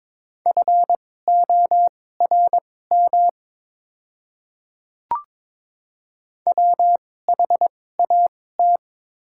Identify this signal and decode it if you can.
Morse code, heard as a keyed tone, about 22 wpm, spaced as for 16 wpm, 700 Hz. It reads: FORM WHAT
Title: 2x Character Spacing - Top 200 Words - 22wpm